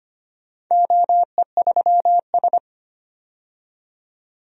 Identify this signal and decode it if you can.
Morse code, heard as a keyed tone, about 25 wpm, 700 Hz. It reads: OE3S
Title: DXCC-100 Call Signs - 4 Characters - 25wpm